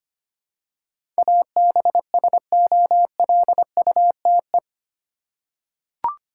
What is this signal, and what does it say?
Morse code, heard as a keyed tone, about 25 wpm, 700 Hz. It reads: ABSOLUTE
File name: Eight Letter Words - 25wpm